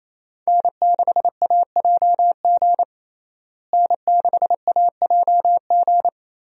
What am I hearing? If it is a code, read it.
Morse code, heard as a keyed tone, about 27 wpm, 700 Hz. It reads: N6AJG N6AJG
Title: USA Call Signs - 5 Characters - Repeated 2x - 28wpm